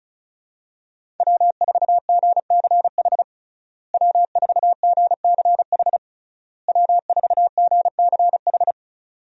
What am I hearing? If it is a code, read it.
Morse code, heard as a keyed tone, about 36 wpm, 700 Hz. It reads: W4GCH W4GCH W4GCH
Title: USA Call Signs - 5 Characters - Repeated 3x - 35wpm